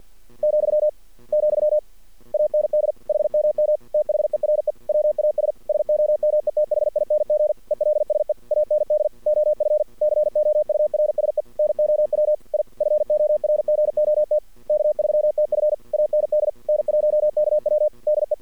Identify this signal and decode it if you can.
Morse code, heard as a keyed tone, about 49 wpm, 600 Hz. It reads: , , AND HMM THERE WAS SOMETHING ELSE AND OF COURSE NOW I FORGOT K3TF AND N1KW DE
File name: archived / CRSnet